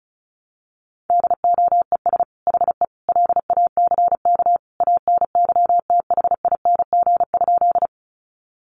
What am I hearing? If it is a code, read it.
Morse code, heard as a keyed tone, about 34 wpm, 700 Hz. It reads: DOES HE LACK ANYTHING?
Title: Sentences from Top 1100 Words 35wpm